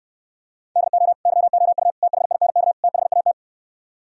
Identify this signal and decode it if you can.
Morse code, heard as a keyed tone, about 68 wpm, 700 Hz. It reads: DO YOU THINK THAT